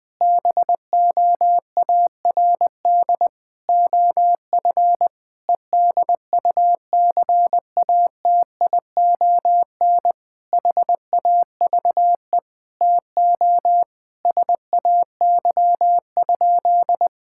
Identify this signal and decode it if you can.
Morse code, heard as a keyed tone, about 20 wpm, 700 Hz. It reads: BOARD OF EDUCATION HAVE TO SAY?
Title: Sentences from Top 1000 Words - Review Part 6 - 20wpm